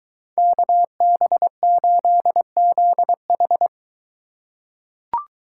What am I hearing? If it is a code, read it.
Morse code, heard as a keyed tone, about 21 wpm, 700 Hz. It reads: KB8ZH